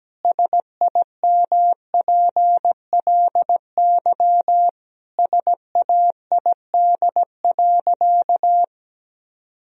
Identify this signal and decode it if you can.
Morse code, heard as a keyed tone, about 17 wpm, 700 Hz. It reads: SIMPLY SAID.
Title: Sentences from Top 1000 Words - Review Part 4 - 17wpm